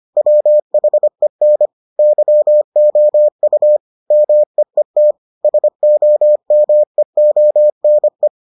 Morse code, heard as a keyed tone, about 25 wpm, 600 Hz. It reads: WHEN YOU MEET SOMEONE